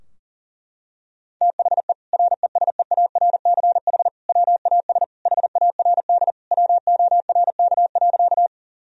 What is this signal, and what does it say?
Morse code, heard as a keyed tone, about 40 wpm, 690 Hz. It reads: THE RESEARCH WAS HARD WORK.
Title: Sentences from Top 1000 Words - Review Part 4 - 40wpm